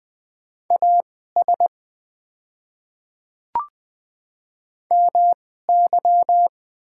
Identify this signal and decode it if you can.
Morse code, heard as a keyed tone, about 20 wpm, spaced as for 14 wpm, 700 Hz. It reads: AS MY
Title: 2x Character Spacing - Top 200 Words - 20wpm